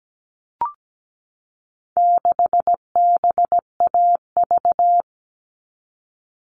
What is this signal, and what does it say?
Morse code, heard as a keyed tone, about 17 wpm, 700 Hz. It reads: 6BAV
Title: All Cut Numbers - 4-Digit 17wpm